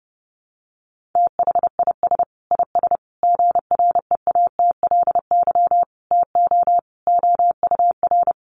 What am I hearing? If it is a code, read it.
Morse code, heard as a keyed tone, about 30 wpm, 700 Hz. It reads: THIS IS GREATLY TO OUR
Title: Sentences from Top 1400 Words 30wpm